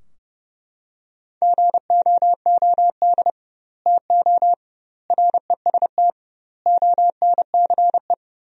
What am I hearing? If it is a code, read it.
Morse code, heard as a keyed tone, about 30 wpm, 700 Hz. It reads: GOOD TO REST ONCE
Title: 2x Word-Spacing - Sentences from Top 500 Words - 30wpm